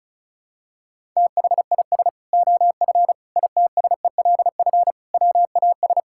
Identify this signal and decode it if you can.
Morse code, heard as a keyed tone, about 34 wpm, 700 Hz. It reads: THIS OF ITSELF WAS
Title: Sentences from Top 1000 Words - Review Part 5 - 35wpm